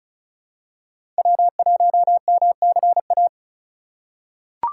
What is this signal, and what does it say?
Morse code, heard as a keyed tone, about 34 wpm, 700 Hz. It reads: W1MCA
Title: USA Call Signs - 5 Characters - 4x Character Spacing - 35wpm